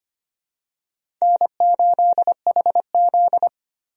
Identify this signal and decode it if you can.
Morse code, heard as a keyed tone, about 25 wpm, 700 Hz. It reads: N8HZ
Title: USA Call Signs - 4 Characters - Repeated 3x - 25wpm